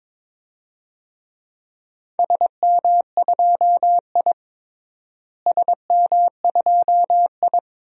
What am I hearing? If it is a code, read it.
Morse code, heard as a keyed tone, about 22 wpm, 700 Hz. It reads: SM2I SM2I
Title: DXCC-100 Call Signs - 4 Characters - Repeated 3x - 22wpm